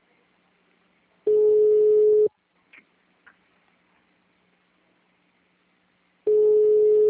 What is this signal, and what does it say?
Ringback tone (Europe, 425 Hz).